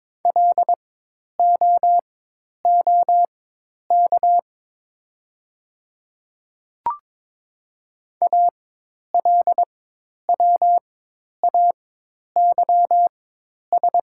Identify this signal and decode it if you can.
Morse code, heard as a keyed tone, about 22 wpm, spaced as for 10 wpm, 700 Hz. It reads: LOOK ALWAYS
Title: 4x Character Spacing - Top 300 Words - 22wpm